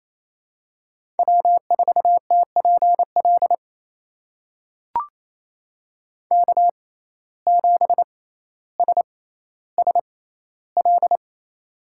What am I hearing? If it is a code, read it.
Morse code, heard as a keyed tone, about 28 wpm, spaced as for 14 wpm, 700 Hz. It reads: W4TPL K7SSL
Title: USA Call Signs - 5 Characters - 6x Character Spacing - 28wpm